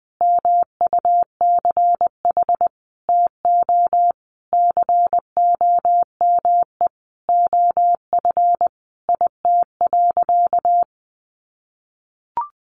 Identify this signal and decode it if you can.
Morse code, heard as a keyed tone, about 20 wpm, 700 Hz. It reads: MUCH TO COME OF IT.
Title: Sentences from Top 1100 Words 20wpm